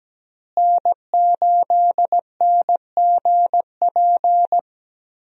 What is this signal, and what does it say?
Morse code, heard as a keyed tone, about 17 wpm, 700 Hz. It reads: N8NGP
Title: USA Call Signs - 5 Characters - Repeated 2x - 17wpm